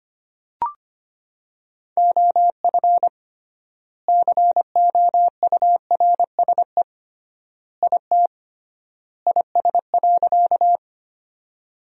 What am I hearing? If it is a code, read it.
Morse code, heard as a keyed tone, about 25 wpm, 700 Hz. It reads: OF COURSE IT IS.
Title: 3x Word-Spacing - Sentences from Top 500 Words - 25wpm